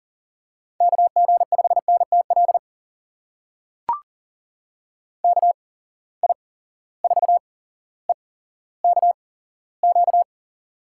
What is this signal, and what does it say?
Morse code, heard as a keyed tone, about 40 wpm, spaced as for 17 wpm, 700 Hz. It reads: KG5NTL KI4EKQ
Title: USA Call Signs - 6 Characters - 8x Character Spacing - 40wpm